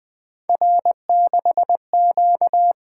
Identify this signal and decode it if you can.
Morse code, heard as a keyed tone, about 20 wpm, 700 Hz. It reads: R6Q
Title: Three Letter-Number Combinations - 20wpm